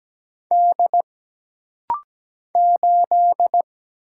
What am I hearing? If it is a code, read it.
Morse code, heard as a keyed tone, about 17 wpm, 700 Hz. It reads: D 8
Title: Single Letter-Number - Rapid-Fire 17wpm